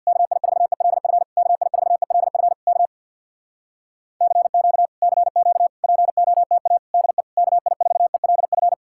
Morse code, heard as a keyed tone, about 62 wpm, 700 Hz. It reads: KI4ELF KI4ELF K CQ CQ POTA DE KI4ELF
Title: Speed-Racing Common QSO Phrases - 50wpm